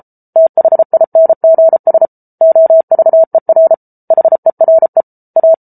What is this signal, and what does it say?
Morse code, heard as a keyed tone, about 34 wpm, 650 Hz. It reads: THINGS OVER HERE A